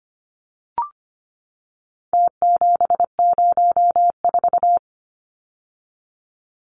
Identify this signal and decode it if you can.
Morse code, heard as a keyed tone, about 25 wpm, 700 Hz. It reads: T704
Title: Numbers - Contest Style - 4-Digit 25wpm